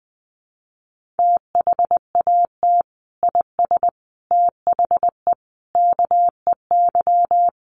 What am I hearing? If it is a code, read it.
Morse code, heard as a keyed tone, about 20 wpm, 700 Hz. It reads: THAT IS THE KEY
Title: Sentences from Top 1400 Words 20wpm